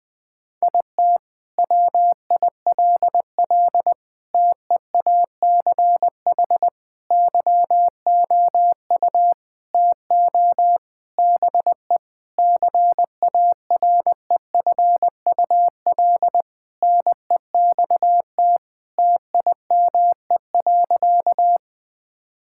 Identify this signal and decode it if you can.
Morse code, heard as a keyed tone, about 21 wpm, 700 Hz. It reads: IT WILL TEACH YOU TO BE CAREFUL NEXT TIME.
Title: Sentences from Top 1600 Words 20wpm